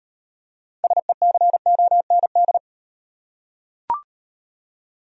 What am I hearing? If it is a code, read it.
Morse code, heard as a keyed tone, about 38 wpm, 700 Hz. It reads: SECOND